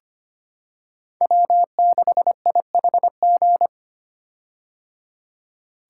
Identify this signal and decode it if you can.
Morse code, heard as a keyed tone, about 25 wpm, 700 Hz. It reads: W6IHG
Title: K1USN SST QSO Practice - 25wpm